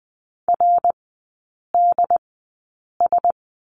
Morse code, heard as a keyed tone, about 20 wpm, spaced as for 8 wpm, 700 Hz. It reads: RDS
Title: 159 Lesson 26e - Letter g - Three Characters 20wpm